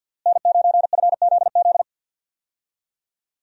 Morse code, heard as a keyed tone, about 48 wpm, 700 Hz. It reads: N9FZZ
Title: USA Call Signs - 5 Characters - Repeated 3x - 50wpm